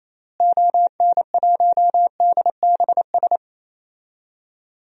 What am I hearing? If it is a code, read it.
Morse code, heard as a keyed tone, about 28 wpm, 700 Hz. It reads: ON1DBS
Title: DXCC-100 Call Signs - 6 Characters - Repeated 2x - 28wpm